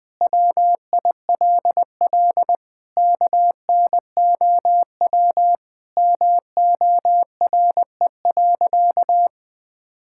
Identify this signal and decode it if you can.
Morse code, heard as a keyed tone, about 21 wpm, 700 Hz. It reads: WILL KNOW MORE.
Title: Sentences with 6-Word Phrases - Spoken Beginning - 20wpm